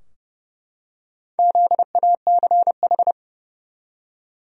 Morse code, heard as a keyed tone, about 30 wpm, 700 Hz. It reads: ZACH
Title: USA Common Names - Max 5 Letters - 30wpm